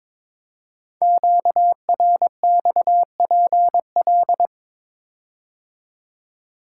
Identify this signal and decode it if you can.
Morse code, heard as a keyed tone, about 23 wpm, 700 Hz. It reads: QRXPL
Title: Code Groups - 5 Letters - 22wpm